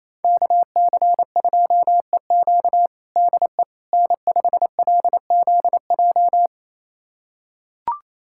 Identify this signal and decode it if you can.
Morse code, heard as a keyed tone, about 28 wpm, 700 Hz. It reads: KC2EQ DE N5LZJ
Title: USA Callsign DE Callsign - 5 Characters - 28wpm